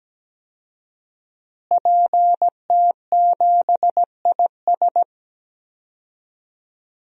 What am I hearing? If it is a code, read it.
Morse code, heard as a keyed tone, about 17 wpm, 700 Hz. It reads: PT7IS
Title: DXCC-100 Call Signs - 5 Characters - Repeated 2x - 17wpm